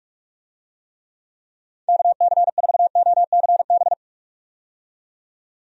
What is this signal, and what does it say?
Morse code, heard as a keyed tone, about 46 wpm, 700 Hz. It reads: KC4OCB
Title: USA Call Signs - 6 Characters - 45wpm